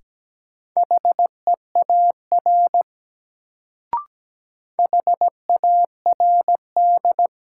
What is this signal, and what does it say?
Morse code, heard as a keyed tone, about 17 wpm, 700 Hz. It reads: HEAR HARD
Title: Sentences from Top 300 Words - New Word Drill - 17wpm